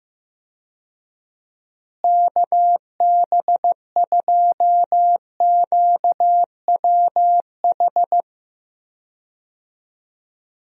Morse code, heard as a keyed tone, about 15 wpm, 700 Hz. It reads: KB2QWH
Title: USA Call Signs - 6 Characters - Repeated 3x - 15wpm